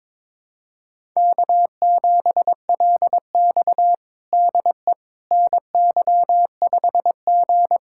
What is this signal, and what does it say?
Morse code, heard as a keyed tone, about 22 wpm, 700 Hz. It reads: K7LX DE NY5G
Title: USA Callsign DE Callsign - 4 Characters - 22wpm